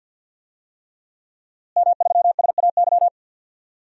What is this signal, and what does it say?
Morse code, heard as a keyed tone, about 50 wpm, 700 Hz. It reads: M3SAY
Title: DXCC-100 Call Signs - 5 Characters - Repeated 3x - 50wpm